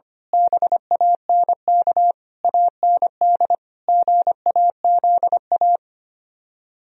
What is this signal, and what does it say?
Morse code, heard as a keyed tone, about 24 wpm, 700 Hz. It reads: BANK AND GAZA
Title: Speed-Racing - Sets of 5 Words 25wpm